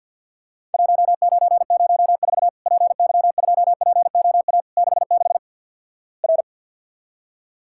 Morse code, heard as a keyed tone, about 53 wpm, 700 Hz. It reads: 1904 PY2POA 66 R
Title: Worked All Europe DX Contest - QTC Exchange with Repeats - 50wpm